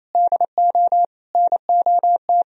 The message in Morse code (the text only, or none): DO NOT